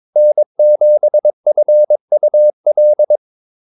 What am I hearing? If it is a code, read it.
Morse code, heard as a keyed tone, about 22 wpm, 600 Hz. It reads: N7FUL
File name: Callsigns-22wpm